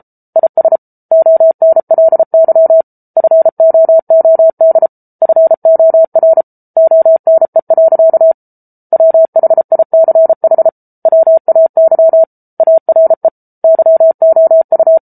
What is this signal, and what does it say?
Morse code, heard as a keyed tone, about 34 wpm, 650 Hz. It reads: IS ONLY FOOD FOR ONE. WHICH WAY ARE YOU